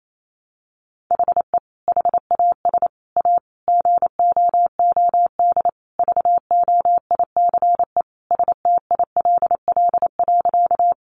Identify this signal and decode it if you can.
Morse code, heard as a keyed tone, about 28 wpm, 700 Hz. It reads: HE HAS A GOOD VOICE STILL.
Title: Sentences from Top 500 Words 28wpm